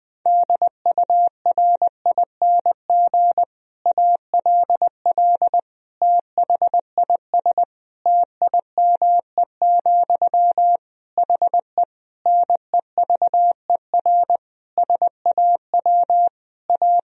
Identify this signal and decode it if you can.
Morse code, heard as a keyed tone, about 21 wpm, 700 Hz. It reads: DURING ALL THIS TIME, HE NEVER SAW A
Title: Sentences from Top 1000 Words - Review Part 3 - 20wpm